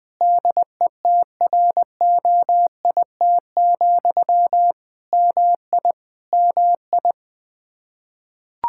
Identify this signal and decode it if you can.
Morse code, heard as a keyed tone, about 20 wpm, 700 Hz. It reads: DETROIT, MI MI
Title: Practice QSO - Exchange 1 - Beginning - 20wpm